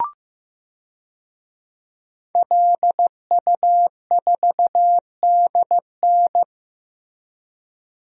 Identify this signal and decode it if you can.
Morse code, heard as a keyed tone, about 15 wpm, 700 Hz. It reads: LU4DN
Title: DXCC-100 Call Signs - 5 Characters - Repeated 3x - 15wpm